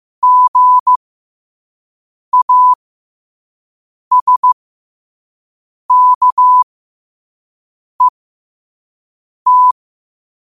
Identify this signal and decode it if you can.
Morse code, heard as a keyed tone, about 15 wpm, spaced as for 5 wpm, 1000 Hz. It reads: GASKET
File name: qotdfiles / 5wpm